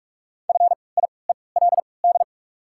Morse code, heard as a keyed tone, about 43 wpm, spaced as for 25 wpm, 700 Hz. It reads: FIELD
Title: 3x Character Spacing - Top 500 Words - 45wpm